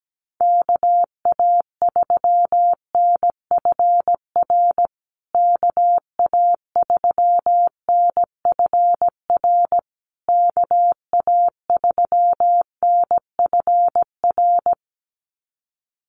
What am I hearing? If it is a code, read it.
Morse code, heard as a keyed tone, about 17 wpm, 700 Hz. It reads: KA3NFR KA3NFR KA3NFR
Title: USA Call Signs Repeated 3x 17wpm